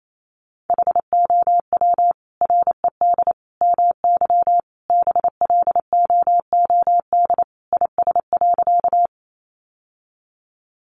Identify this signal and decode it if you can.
Morse code, heard as a keyed tone, about 28 wpm, 700 Hz. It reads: HOW RED MY BLOOD IS.
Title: Sentences from Top 1100 Words 28wpm